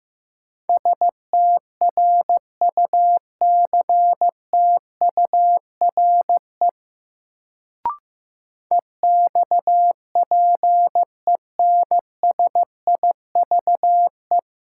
Morse code, heard as a keyed tone, about 15 wpm, 700 Hz. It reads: STRUCTURE EXPENSIVE
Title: Sentences from Top 1500 Words - New Word Drill Alt - 15wpm